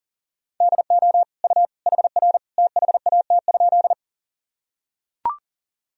Morse code, heard as a keyed tone, about 40 wpm, 700 Hz. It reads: DO U HR THAT?
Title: Sentences from Top 300 Words and Abbreviations 40wpm